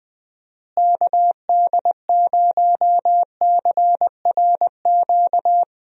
Morse code, heard as a keyed tone, about 20 wpm, 700 Hz. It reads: KD0CRQ
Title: USA Call Signs - 6 Characters - Repeated 3x - 20wpm